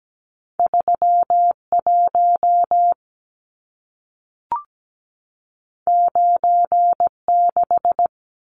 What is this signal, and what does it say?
Morse code, heard as a keyed tone, about 17 wpm, 700 Hz. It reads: 31 96
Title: Numbers - 2-Digit 17wpm